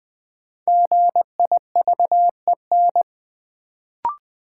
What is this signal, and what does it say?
Morse code, heard as a keyed tone, about 21 wpm, 700 Hz. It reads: GIVEN
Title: Sentences from Top 1400 Words - New Word Drill Alt - 20wpm